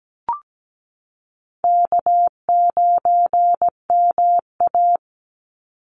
Morse code, heard as a keyed tone, about 17 wpm, 700 Hz. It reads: K9MA
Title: Call Signs Repeated 2x - 17wpm